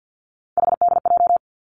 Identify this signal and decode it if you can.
Morse code, heard as a keyed tone, about 50 wpm, 700 Hz. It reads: HDJ
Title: Three Challenging Letter-Number Combinations 50wpm